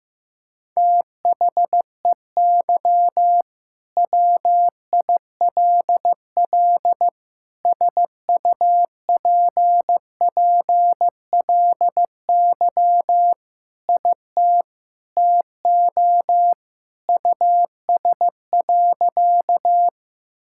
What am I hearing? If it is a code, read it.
Morse code, heard as a keyed tone, about 15 wpm, 700 Hz. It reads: THEY WILL SUPPLY IT TO US.
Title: Sentences from Top 1000 Words - Review Part 5 - 15wpm